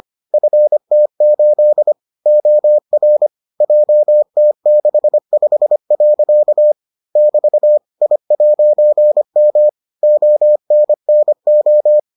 Morse code, heard as a keyed tone, about 25 wpm, 600 Hz. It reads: FT8 OR JT65. <BT> I'M ONNO